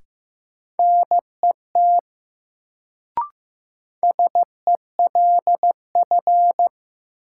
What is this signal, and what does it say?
Morse code, heard as a keyed tone, about 15 wpm, 700 Hz. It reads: NET SELF
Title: Sentences from Top 1400 Words - New Word Drill Alt - 15wpm